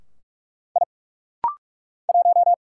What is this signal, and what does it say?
Morse code, heard as a keyed tone, about 43 wpm, 690 Hz. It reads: I 1
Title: Single Letter-Number - Rapid-Fire 45wpm